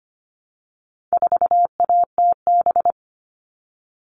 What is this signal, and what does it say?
Morse code, heard as a keyed tone, about 25 wpm, 700 Hz. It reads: 4ATB
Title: All Cut Numbers - 4-Digit 25wpm